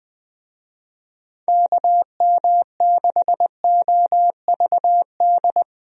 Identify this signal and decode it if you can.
Morse code, heard as a keyed tone, about 20 wpm, 700 Hz. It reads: KM6OVD
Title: USA Call Signs - 6 Characters - 20wpm